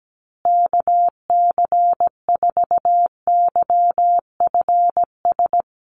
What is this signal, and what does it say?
Morse code, heard as a keyed tone, about 17 wpm, 700 Hz. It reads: KC4YFS
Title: USA Call Signs Repeated 3x 17wpm